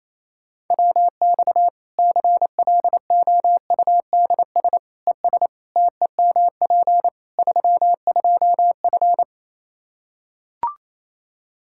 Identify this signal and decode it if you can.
Morse code, heard as a keyed tone, about 27 wpm, 700 Hz. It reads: WX CLOUDS ES TEMP 32F
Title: Practice QSO - Exchange 2 - Info - 28wpm